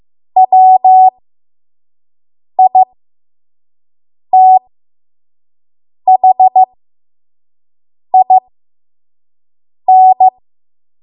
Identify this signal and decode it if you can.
Morse code, heard as a keyed tone, about 15 wpm, spaced as for 5 wpm, 750 Hz. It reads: WITHIN
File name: Morse / 05WPM